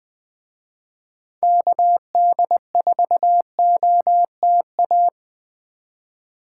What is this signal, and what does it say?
Morse code, heard as a keyed tone, about 20 wpm, 700 Hz. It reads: KD4OTA
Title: Speed Racing - USA 2x3 Call Signs 17wpm